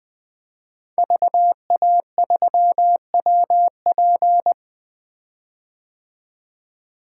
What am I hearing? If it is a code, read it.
Morse code, heard as a keyed tone, about 20 wpm, 700 Hz. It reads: VA3WP
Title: 2023 POTA Activator Call Signs - 20wpm